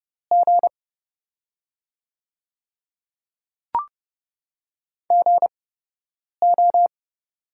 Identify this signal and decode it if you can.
Morse code, heard as a keyed tone, about 30 wpm, spaced as for 8 wpm, 700 Hz. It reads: G GO
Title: 8x Character Spacing - Top 500 Words - 30wpm